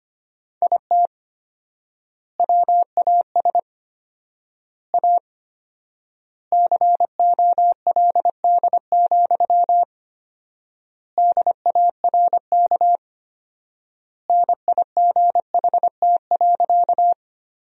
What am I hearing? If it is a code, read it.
Morse code, heard as a keyed tone, about 25 wpm, 700 Hz. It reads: IT WAS A COLD, DARK NIGHT.
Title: 4x Word-Spacing - Sentences from Top 500 Words - 25wpm